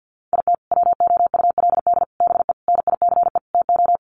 Morse code, heard as a keyed tone, about 50 wpm, 700 Hz. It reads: IT WOULD BE NICE TO
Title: Sets of 5 Words Encore2 50wpm